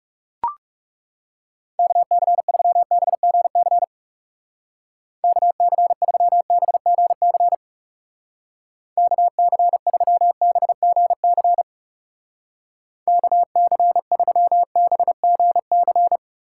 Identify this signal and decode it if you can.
Morse code, heard as a keyed tone, about 38 wpm, 700 Hz. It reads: KC3BGC KC3BGC KC3BGC KC3BGC